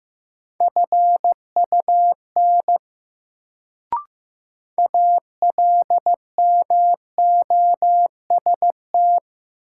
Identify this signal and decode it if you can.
Morse code, heard as a keyed tone, about 15 wpm, 700 Hz. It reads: FUN ALMOST
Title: Sentences from Top 700 Words - New Word Drill - 15wpm